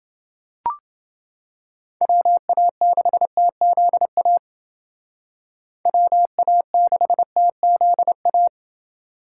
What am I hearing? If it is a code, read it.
Morse code, heard as a keyed tone, about 28 wpm, 700 Hz. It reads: WA6TZA WA6TZA